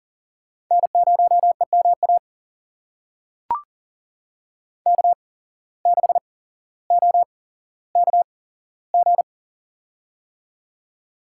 Morse code, heard as a keyed tone, about 40 wpm, spaced as for 17 wpm, 700 Hz. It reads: N0EMA K6OKG
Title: USA Call Signs - 5 Characters - 8x Character Spacing - 40wpm